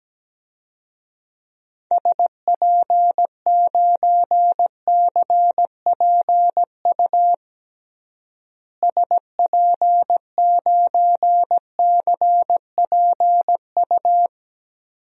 Morse code, heard as a keyed tone, about 17 wpm, 700 Hz. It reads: SP9CPU SP9CPU
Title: DXCC-100 Call Signs - 6 Characters - Repeated 3x - 17wpm